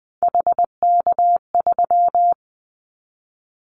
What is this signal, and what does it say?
Morse code, heard as a keyed tone, about 20 wpm, 700 Hz. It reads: HK3
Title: Callsign Prefixes for Top 100 DXCC - 20wpm